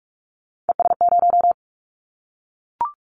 Morse code, heard as a keyed tone, about 46 wpm, 700 Hz. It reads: ES0
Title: Callsign Prefixes for Top 50 DXCC - 45wpm